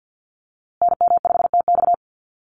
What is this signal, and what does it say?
Morse code, heard as a keyed tone, about 50 wpm, 700 Hz. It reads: NM5TX
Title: USA Call Signs Repeated 2x 50wpm